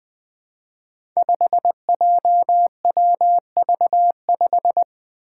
Morse code, heard as a keyed tone, about 20 wpm, 700 Hz. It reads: 5JWV5
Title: Five Challenging Letter-Number Combinations 20wpm